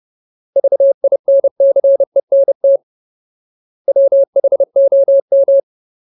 Morse code, heard as a keyed tone, about 30 wpm, 550 Hz. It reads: VINCENT WHOM
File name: Ultimate Word List / 30WPM Ultimate